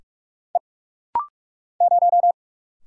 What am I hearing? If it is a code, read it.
Morse code, heard as a keyed tone, about 44 wpm, 700 Hz. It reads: E 0
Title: Single Letter-Number - Rapid-Fire 45wpm